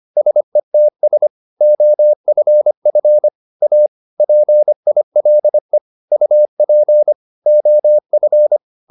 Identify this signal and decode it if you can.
Morse code, heard as a keyed tone, about 25 wpm, 600 Hz. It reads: SETS OFF A PILE UP OF